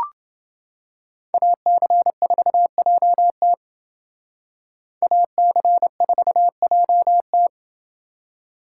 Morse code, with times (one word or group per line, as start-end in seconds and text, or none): 1.3-3.5 AC4JT
5.0-7.5 AC4JT